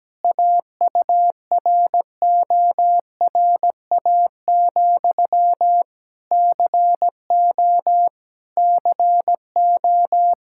Morse code, heard as a keyed tone, about 17 wpm, 700 Hz. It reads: AURORA, CO CO
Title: Practice QSO - Exchange 1 - Beginning - 17wpm